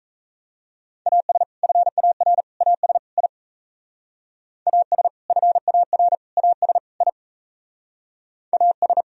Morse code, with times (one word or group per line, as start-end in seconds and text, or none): 1.1-1.4 AS
1.6-2.4 FAR
2.6-3.0 AS
3.2-3.3 I
4.7-5.1 AS
5.3-6.2 FAR
6.4-6.8 AS
7.0-7.1 I
8.5-9.0 AS